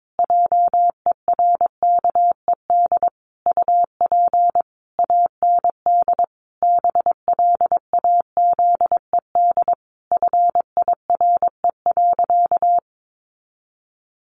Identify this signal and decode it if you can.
Morse code, heard as a keyed tone, about 22 wpm, 700 Hz. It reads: JERKED UP AND BLAZED FIRE.
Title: A Sound Of Thunder 22wpm